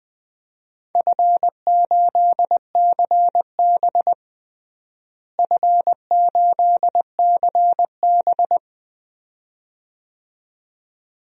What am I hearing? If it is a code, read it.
Morse code, heard as a keyed tone, about 20 wpm, 700 Hz. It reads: F8CB F8CB
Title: DXCC-100 Call Signs - 4 Characters - Repeated 2x - 20wpm